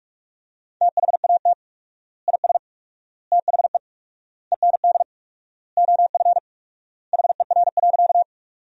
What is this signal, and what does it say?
Morse code, heard as a keyed tone, about 46 wpm, 700 Hz. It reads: THAT IS THE END OF HER.
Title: 4x Word-Spacing - Sentences from Top 200 Words - 45wpm